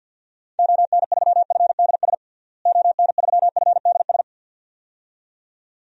Morse code, heard as a keyed tone, about 46 wpm, 700 Hz. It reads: ON3FDS ON3FDS
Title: DXCC-100 Call Signs - 6 Characters - Repeated 2x - 50wpm